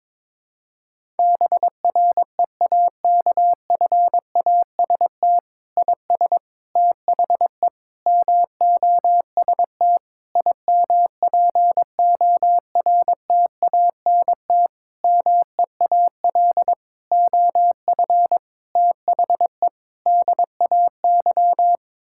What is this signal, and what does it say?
Morse code, heard as a keyed tone, about 22 wpm, 700 Hz. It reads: BREAKFAST IS THE MOST IMPORTANT MEAL OF THE DAY
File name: Sets of 9 Words - 22wpm